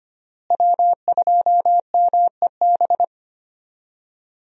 Morse code, heard as a keyed tone, about 25 wpm, 700 Hz. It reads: W2MEB
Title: USA Call Signs - 5 Characters - Repeated 2x - 25wpm